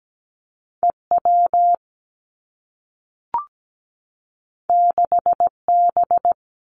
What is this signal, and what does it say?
Morse code, heard as a keyed tone, about 17 wpm, 700 Hz. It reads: E W 6 B
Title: Two Challenging Letter-Number Combinations 17wpm